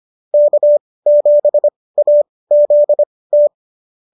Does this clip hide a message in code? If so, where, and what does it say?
Yes: Morse code from 0.3 to 3.5 s, heard as a keyed tone, about 25 wpm, spaced as for 18 wpm, 590 Hz. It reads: K7AZT